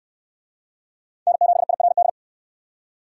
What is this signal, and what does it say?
Morse code, heard as a keyed tone, about 61 wpm, 700 Hz. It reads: N7IRD